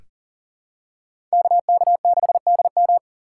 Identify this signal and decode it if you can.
Morse code, heard as a keyed tone, about 40 wpm, 700 Hz. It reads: KK6DM
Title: USA Call Signs - 5 Characters - 2x Character Spacing - 40wpm